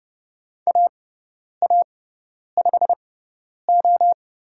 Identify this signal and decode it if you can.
Morse code, heard as a keyed tone, about 30 wpm, spaced as for 10 wpm, 700 Hz. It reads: AA5O
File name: Lesson 16g - Letter c - Call Signs 30wpm